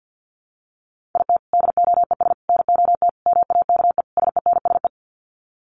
Morse code, heard as a keyed tone, about 50 wpm, 700 Hz. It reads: IT DOES NOT MAKE SENSE
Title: Sets of 5 Words Encore2 50wpm